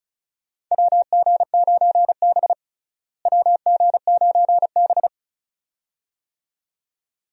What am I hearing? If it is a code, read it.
Morse code, heard as a keyed tone, about 36 wpm, 700 Hz. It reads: WG9B WG9B
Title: USA Call Signs - 4 Characters - Repeated 2x - 35wpm